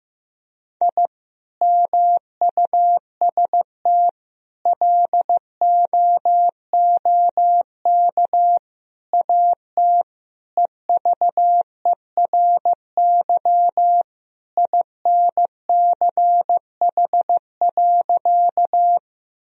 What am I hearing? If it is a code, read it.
Morse code, heard as a keyed tone, about 15 wpm, 700 Hz. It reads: I MUST LOOK AT EVERY INCH.
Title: Sentences from Top 1000 Words - Review Part 3 - 15wpm